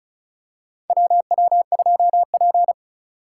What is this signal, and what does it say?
Morse code, heard as a keyed tone, about 36 wpm, 700 Hz. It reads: WW2P
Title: K1USN SST QSO Practice - 35wpm